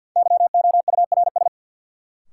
Morse code, heard as a keyed tone, about 53 wpm, 700 Hz. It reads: YOURS
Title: Sentences from Top 1500 Words - New Word Drill - 50wpm